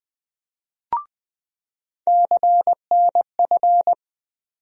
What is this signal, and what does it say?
Morse code, heard as a keyed tone, about 21 wpm, 700 Hz. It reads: CNF
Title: USA Call Sign Postfixes - 3 Letters 20wpm